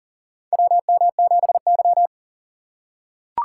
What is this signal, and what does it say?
Morse code, heard as a keyed tone, about 40 wpm, 700 Hz. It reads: WM7Y
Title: USA Call Signs - 4 Characters - 2x Character Spacing - 40wpm